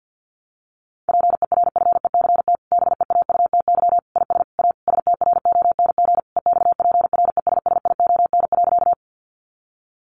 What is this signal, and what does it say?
Morse code, heard as a keyed tone, about 50 wpm, 700 Hz. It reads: PERFECT BEAUTY IS A STRONG EXPRESSION.
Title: Sentences from Top 1200 Words 50wpm